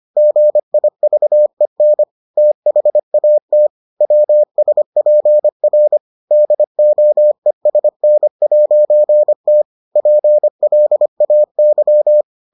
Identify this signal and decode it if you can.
Morse code, heard as a keyed tone, about 25 wpm, 600 Hz. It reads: GIVEN THAT WSPR DOESN'T PLAY